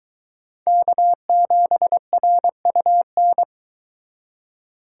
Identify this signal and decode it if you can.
Morse code, heard as a keyed tone, about 23 wpm, 700 Hz. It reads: K7RUN